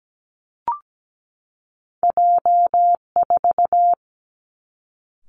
Morse code, heard as a keyed tone, about 17 wpm, 700 Hz. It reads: J 4
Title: Two Challenging Letter-Number Combinations 17wpm